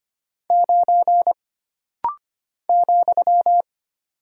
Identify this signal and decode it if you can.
Morse code, heard as a keyed tone, about 25 wpm, 700 Hz. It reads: 9 ,
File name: Single Character (Letter:Number:Punctuation:Prosign) - Rapid-Fire - 25wpm